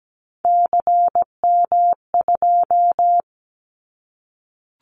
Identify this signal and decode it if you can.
Morse code, heard as a keyed tone, about 17 wpm, 700 Hz. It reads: CM2
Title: Callsign Prefixes for Top 100 DXCC - 17wpm